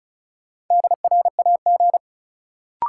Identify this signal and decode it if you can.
Morse code, heard as a keyed tone, about 36 wpm, 700 Hz. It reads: DRAG
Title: Sentences from Top 1500 Words - New Word Drill Alt - 35wpm